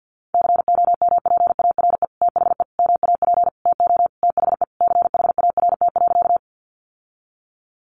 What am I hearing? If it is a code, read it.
Morse code, heard as a keyed tone, about 50 wpm, 700 Hz. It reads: COMPARE THE MAP TO THE CHART.
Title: Sentences from Top 1300 Words 50wpm